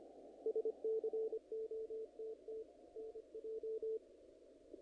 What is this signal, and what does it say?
Morse code, heard as a keyed tone, about 25 wpm, 440 Hz. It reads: SCOTT NJ
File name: contests / naqp-cw-k5tr-op-k5pi- -1900